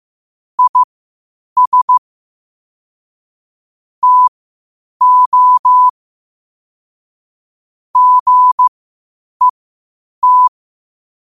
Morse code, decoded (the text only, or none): IS TO GET